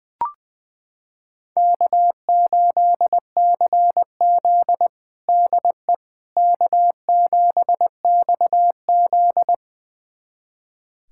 Morse code, heard as a keyed tone, about 20 wpm, 700 Hz. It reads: K8CZ DE K7XZ
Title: USA Callsign DE Callsign - 4 Characters - 20wpm